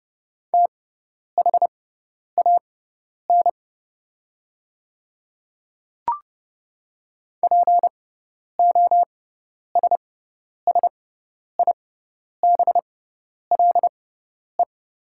Morse code, heard as a keyed tone, about 30 wpm, spaced as for 10 wpm, 700 Hz. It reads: THAN POSSIBLE
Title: 6x Character Spacing - Top 500 Words - 30wpm